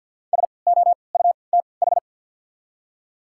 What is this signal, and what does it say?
Morse code, heard as a keyed tone, about 52 wpm, spaced as for 29 wpm, 690 Hz. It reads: SOUTH
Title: 3x Character Spacing - Top 500 Words - 50wpm